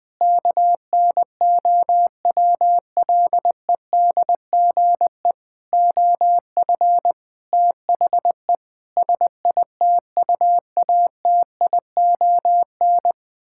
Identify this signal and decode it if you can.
Morse code, heard as a keyed tone, about 21 wpm, 700 Hz. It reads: KNOWLEDGE OF THE SITUATION
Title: Sentences with 9-Word Phrases - 20wpm